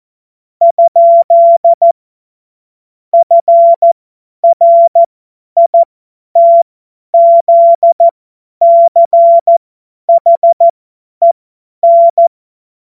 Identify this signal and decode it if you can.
Morse code, heard as a keyed tone, about 14 wpm, spaced as for 10 wpm, 680 Hz. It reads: ? FRITZCHEN
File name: Morsekurs / Witze / Witz 03. August 2021, 50 F70